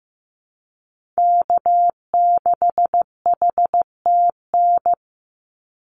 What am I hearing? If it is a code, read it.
Morse code, heard as a keyed tone, about 15 wpm, 700 Hz. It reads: K6HTN
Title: Call Signs Repeated 3x - 15wpm